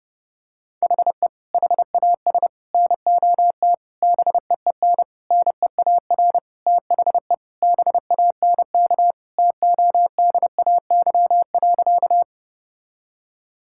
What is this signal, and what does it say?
Morse code, heard as a keyed tone, about 30 wpm, 700 Hz. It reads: HE HAS NOT BEEN NEAR THE BANK TODAY.
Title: Sentences from Top 1000 Words - Review Part 6 - 30wpm